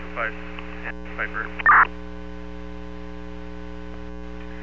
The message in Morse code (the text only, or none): none